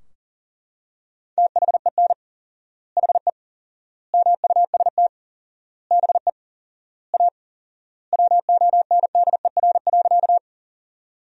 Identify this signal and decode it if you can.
Morse code, heard as a keyed tone, about 40 wpm, 700 Hz. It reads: THEN HE MUST BE A WONDER.
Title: 4x Word-Spacing - Sentences from Top 500 Words - 40wpm